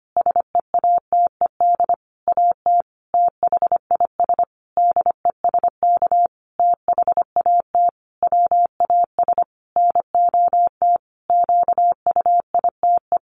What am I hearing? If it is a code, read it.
Morse code, heard as a keyed tone, about 25 wpm, 700 Hz. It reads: SEATED AT THIS DESK THAT WAS NOT QUITE
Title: A Sound Of Thunder 25wpm